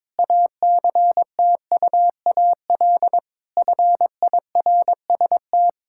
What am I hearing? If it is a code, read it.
Morse code, heard as a keyed tone, about 22 wpm, 700 Hz. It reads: ACTUAL FIRST
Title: Sentences from Top 1000 Words - Review Part 5 - 22wpm